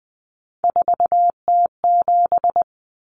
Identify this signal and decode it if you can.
Morse code, heard as a keyed tone, about 20 wpm, 700 Hz. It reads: 4T7
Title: Common Cut Numbers - 3-Digit 20wpm